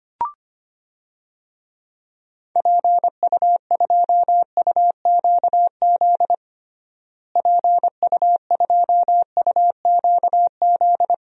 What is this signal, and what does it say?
Morse code, heard as a keyed tone, about 25 wpm, 700 Hz. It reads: PU2UQZ PU2UQZ
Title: DXCC-100 Call Signs - 6 Characters - Repeated 3x - 25wpm